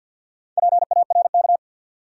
Morse code, heard as a keyed tone, about 50 wpm, 700 Hz. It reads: PARK
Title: Sentences from Top 1300 Words - New Word Drill Alt - 50wpm